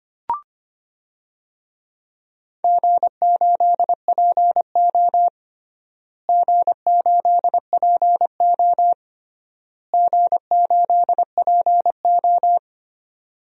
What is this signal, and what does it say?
Morse code, heard as a keyed tone, about 25 wpm, 700 Hz. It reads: G8PO G8PO G8PO
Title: DXCC-100 Call Signs - 4 Characters - Repeated 3x - 25wpm